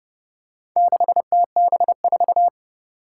Morse code, heard as a keyed tone, about 30 wpm, 700 Hz. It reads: 6TB4
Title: Four Challenging Letter-Number Combinations 30wpm